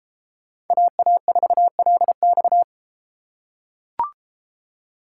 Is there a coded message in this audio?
Yes: Morse code from 0.7 to 2.6 s, heard as a keyed tone, about 33 wpm, 700 Hz. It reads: AA4LX